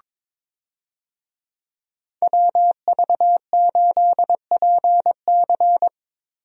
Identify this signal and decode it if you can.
Morse code, heard as a keyed tone, about 22 wpm, 700 Hz. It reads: WV8PC
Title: USA Call Signs - 5 Characters - Repeated 3x - 22wpm